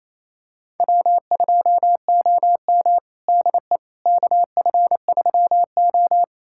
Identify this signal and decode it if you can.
Morse code, heard as a keyed tone, about 28 wpm, 700 Hz. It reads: W2OM DE KF3O
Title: USA Callsign DE Callsign - 4 Characters - 28wpm